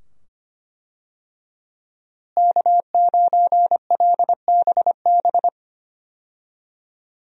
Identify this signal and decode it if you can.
Morse code, heard as a keyed tone, about 25 wpm, 700 Hz. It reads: K9LBB
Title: USA Call Signs - 5 Characters - Repeated 2x - 25wpm